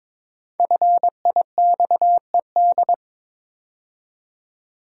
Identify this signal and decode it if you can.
Morse code, heard as a keyed tone, about 22 wpm, 690 Hz. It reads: FIXED
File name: 5 Letter Words - First 3 Letters Given - 22wpm